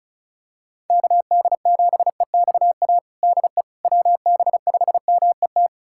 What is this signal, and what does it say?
Morse code, heard as a keyed tone, about 36 wpm, 700 Hz. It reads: KD7EXA DE WB5MET
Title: USA Callsign DE Callsign - 6 Characters - 35wpm